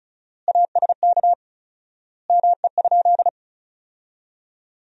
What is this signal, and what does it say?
Morse code, heard as a keyed tone, about 34 wpm, 700 Hz. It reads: ASK ME?
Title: 4x Word-Spacing - Sentences from Top 200 Words - 35wpm